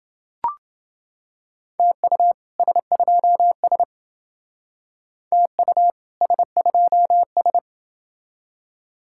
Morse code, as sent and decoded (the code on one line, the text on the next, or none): - ..- / ... ..--- ... / - ..- / ... ..--- ...
TU S2S TU S2S